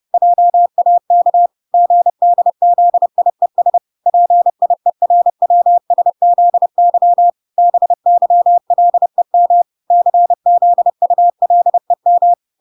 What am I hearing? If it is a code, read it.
Morse code, heard as a keyed tone, about 30 wpm, 700 Hz. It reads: JAK GDZIES PIERWSZY BYLEM CZULEM